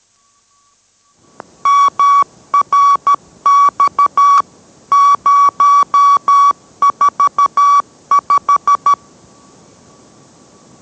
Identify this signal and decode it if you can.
Morse code, heard as a keyed tone, about 13 wpm, 1140 Hz. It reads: MRX 045